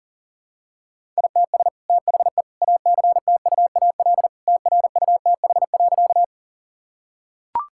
Morse code, heard as a keyed tone, about 40 wpm, 700 Hz. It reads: ITS THE ACTUAL TRUTH.
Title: Sentences from Top 1000 Words - Review Part 5 - 40wpm